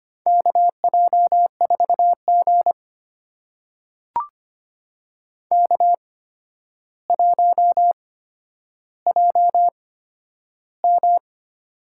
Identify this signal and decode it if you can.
Morse code, heard as a keyed tone, about 25 wpm, spaced as for 11 wpm, 700 Hz. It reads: KJ4G K1JM
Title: USA Call Signs - 4 Characters - 8x Character Spacing - 25wpm